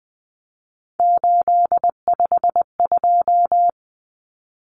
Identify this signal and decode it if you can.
Morse code, heard as a keyed tone, about 20 wpm, 700 Hz. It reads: 852
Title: Common Cut Numbers - 3-Digit 20wpm